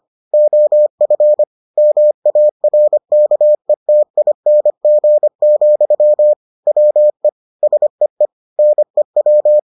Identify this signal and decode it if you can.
Morse code, heard as a keyed tone, about 25 wpm, 600 Hz. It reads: OF MARKETING, WE SEE NEW